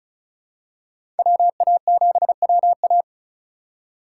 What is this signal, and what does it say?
Morse code, heard as a keyed tone, about 36 wpm, 700 Hz. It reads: WA7WA
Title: USA Call Signs - 5 Characters - 4x Character Spacing - 35wpm